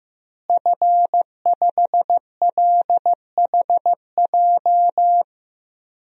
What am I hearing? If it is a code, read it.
Morse code, heard as a keyed tone, about 16 wpm, 700 Hz. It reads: F5LHJ
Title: DXCC-100 Call Signs - 5 Characters - Repeated 3x - 15wpm